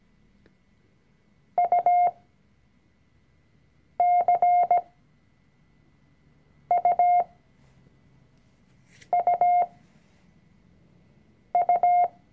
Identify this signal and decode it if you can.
Morse code, heard as a keyed tone, about 17 wpm, spaced as for 4 wpm, 710 Hz. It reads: UCUUU
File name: CW-Online-Kurs / Anfänger / Lektion 004